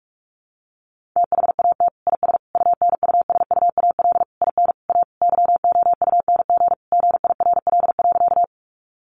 Morse code, heard as a keyed tone, about 44 wpm, 700 Hz. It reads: THAT IS UNUSUAL IN A YOUNG GIRL.
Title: Sentences from Top 1400 Words 45wpm